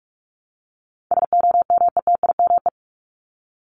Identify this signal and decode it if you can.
Morse code, heard as a keyed tone, about 44 wpm, 700 Hz. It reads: SOMETIME
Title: ES:S Suffix - Top 100 Words with Base Words 45wpm